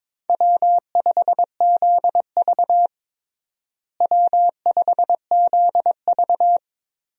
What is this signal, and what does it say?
Morse code, heard as a keyed tone, about 22 wpm, 700 Hz. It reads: W5ZV W5ZV
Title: USA Call Signs - 4 Characters - Repeated 2x - 22wpm